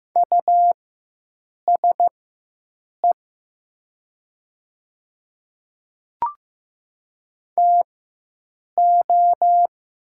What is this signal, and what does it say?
Morse code, heard as a keyed tone, about 15 wpm, spaced as for 7 wpm, 700 Hz. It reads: USE TO
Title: 4x Character Spacing - Top 300 Words - 15wpm